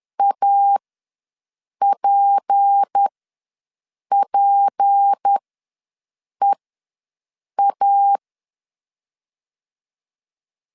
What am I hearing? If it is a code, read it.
Morse code, heard as a keyed tone, about 10 wpm, spaced as for 6 wpm, 800 Hz. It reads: APPEA
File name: LEZIONI / 6 PKX0 / 6 P ws 10wpm 3751